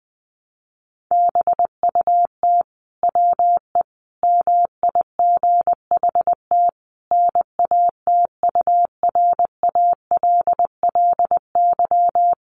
Morse code, heard as a keyed tone, about 19 wpm, 700 Hz. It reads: BUT WE MIGHT NATURALLY
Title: Sentences from Top 1300 Words 20wpm